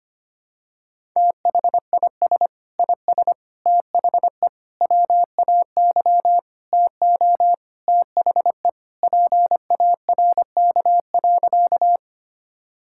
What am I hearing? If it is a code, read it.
Morse code, heard as a keyed tone, about 25 wpm, 700 Hz. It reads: THIS IS THE WAY TO THE PARK.
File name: practice / 25wpm / Sentences with 4-Word Phrases - 25wpm